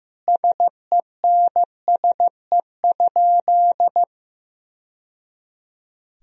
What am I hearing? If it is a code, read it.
Morse code, heard as a keyed tone, about 16 wpm, 700 Hz. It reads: SENSE?
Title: Sentences from Top 1000 Words - Review Part 4 - 15wpm